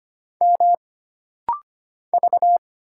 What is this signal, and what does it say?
Morse code, heard as a keyed tone, about 25 wpm, 700 Hz. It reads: M V
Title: Single Letter-Number - Rapid-Fire 25wpm